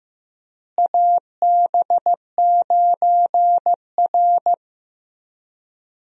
Morse code, heard as a keyed tone, about 15 wpm, 700 Hz. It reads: AB9R
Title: USA Call Signs - 4 Characters - Repeated 3x - 15wpm